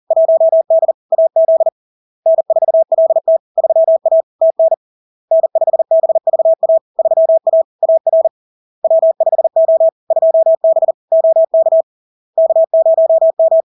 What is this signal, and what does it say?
Morse code, heard as a keyed tone, about 40 wpm, 650 Hz. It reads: 1D AZ N4LT 3A TN N5BVA 3A AR W5O 2B OK K0M
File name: LARK-FD-2022-Log-in-CW-40wpm-650-Hz